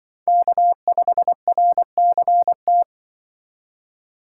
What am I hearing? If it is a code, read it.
Morse code, heard as a keyed tone, about 24 wpm, 700 Hz. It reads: K5RCT
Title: Speed Racing - USA Call Signs 20wpm